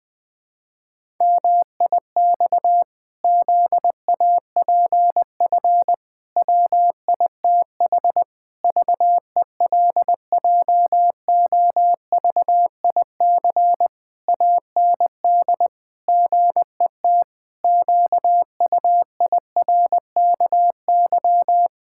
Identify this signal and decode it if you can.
Morse code, heard as a keyed tone, about 19 wpm, 700 Hz. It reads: MIX ZAPF WITH VELJOVIC AND GET QUIRKY
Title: Pangram Challenge v2 - 20wpm